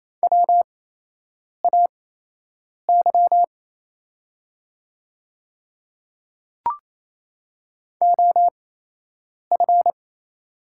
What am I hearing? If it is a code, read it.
Morse code, heard as a keyed tone, about 28 wpm, spaced as for 8 wpm, 700 Hz. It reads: WAY OF
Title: 8x Character Spacing - Top 100 Words - 28wpm